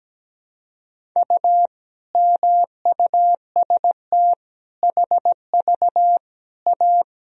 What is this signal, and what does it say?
Morse code, heard as a keyed tone, about 17 wpm, 700 Hz. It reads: U MUST HV A
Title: Sentences from Top 500 Words and Abbreviations 17wpm